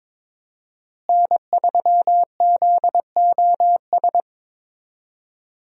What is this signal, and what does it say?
Morse code, heard as a keyed tone, about 22 wpm, 700 Hz. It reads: N3ZOS
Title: USA Call Signs - 5 Characters - Repeated 2x - 22wpm